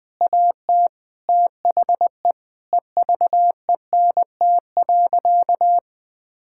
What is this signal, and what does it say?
Morse code, heard as a keyed tone, about 20 wpm, 700 Hz. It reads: AT THE EVENT.
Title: Sentences with 5-Word Phrases - 20wpm